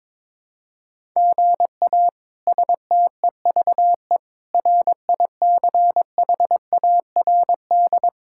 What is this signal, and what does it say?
Morse code, heard as a keyed tone, about 22 wpm, 700 Hz. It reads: GA STEVE RICHARD
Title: K1USN SST QSO Practice - 22wpm